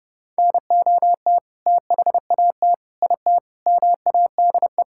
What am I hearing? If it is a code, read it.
Morse code, heard as a keyed tone, about 30 wpm, 700 Hz. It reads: NOT THAT IT MADE
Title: Sentences from Top 1000 Words - Review Part 5 - 30wpm